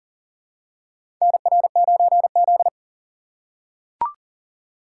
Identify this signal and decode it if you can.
Morse code, heard as a keyed tone, about 40 wpm, 700 Hz. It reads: NR9Z
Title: USA Call Signs - 4x Character Spacing - 40wpm